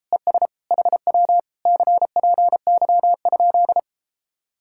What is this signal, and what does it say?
Morse code, heard as a keyed tone, about 37 wpm, 700 Hz. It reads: ES HW CPY?